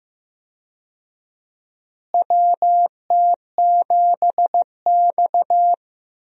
Morse code, heard as a keyed tone, about 16 wpm, 700 Hz. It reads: WT7X
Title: USA Call Signs - 4 Characters - Repeated 3x - 15wpm